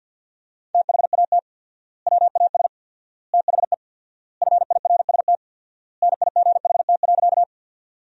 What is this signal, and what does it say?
Morse code, heard as a keyed tone, about 50 wpm, 700 Hz. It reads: THAT WAS THE FIRST NIGHT.
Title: 4x Word-Spacing - Sentences from Top 300 Words - 50wpm